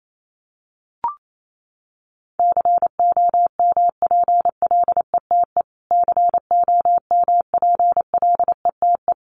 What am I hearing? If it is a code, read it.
Morse code, heard as a keyed tone, about 28 wpm, 700 Hz. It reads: COMPLETE COMPLETE
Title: Top 500 Words Repeated 28wpm